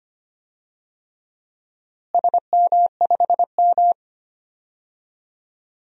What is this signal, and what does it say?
Morse code, heard as a keyed tone, about 25 wpm, 700 Hz. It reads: SM5M
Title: DXCC-100 Call Signs - 4 Characters - Repeated 3x - 25wpm